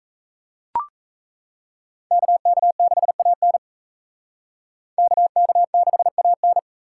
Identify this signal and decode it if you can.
Morse code, heard as a keyed tone, about 36 wpm, 700 Hz. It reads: KK6AN KK6AN